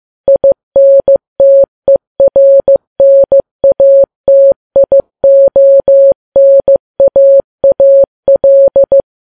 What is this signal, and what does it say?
Morse code, heard as a keyed tone, about 15 wpm, 550 Hz. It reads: INTERNATIONAAL
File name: Files Web tls / Morse on9fv / morse15kps